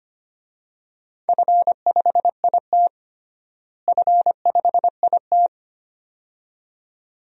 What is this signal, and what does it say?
Morse code, heard as a keyed tone, about 25 wpm, 700 Hz. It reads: F5IT F5IT
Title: DXCC-100 Call Signs - 4 Characters - Repeated 2x - 25wpm